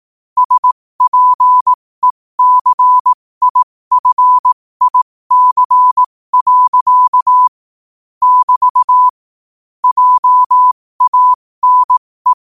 Morse code, heard as a keyed tone, about 18 wpm, spaced as for 15 wpm, 1000 Hz. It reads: SPECIFIC. <BT> JANE
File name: qotdfiles / 15wpm